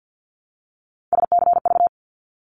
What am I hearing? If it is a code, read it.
Morse code, heard as a keyed tone, about 50 wpm, 700 Hz. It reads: SCV
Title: ARRL Sections 50wpm